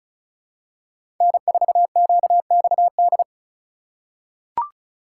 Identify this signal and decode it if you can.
Morse code, heard as a keyed tone, about 34 wpm, 700 Hz. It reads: N4QXD
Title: USA Call Signs - 5 Characters - 3x Character Spacing - 35wpm